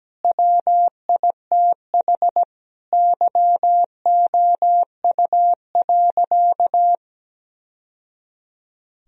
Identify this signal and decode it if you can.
Morse code, heard as a keyed tone, about 17 wpm, 700 Hz. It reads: WITH YOU.
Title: Sentences from Top 1000 Words - Review Part 4 - 17wpm